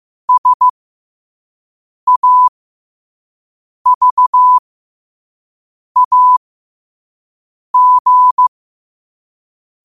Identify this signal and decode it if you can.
Morse code, heard as a keyed tone, about 15 wpm, spaced as for 5 wpm, 1000 Hz. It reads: SAVAG
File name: qotdfiles / 5wpm